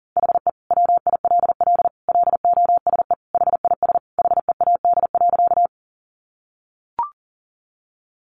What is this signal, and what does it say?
Morse code, heard as a keyed tone, about 40 wpm, 700 Hz. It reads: HE WILL LOSE HIS HEAD.
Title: Sentences from Top 1100 Words 40wpm